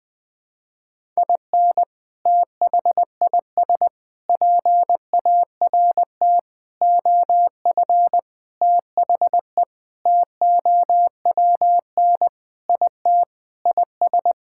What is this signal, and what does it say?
Morse code, heard as a keyed tone, about 21 wpm, 700 Hz. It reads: IN THIS PART OF THE TOWN IT IS
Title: Sentences with 5-Word Phrases - Spoken Beginning - 20wpm